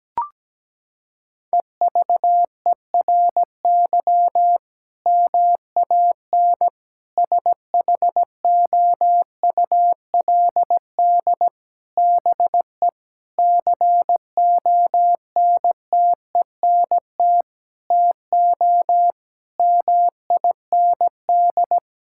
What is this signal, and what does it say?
Morse code, heard as a keyed tone, about 17 wpm, 700 Hz. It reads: EVERY MAN SHOULD BE CONTENT TO MIND
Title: Sentences from Top 1000 Words - Review Part 6 - 17wpm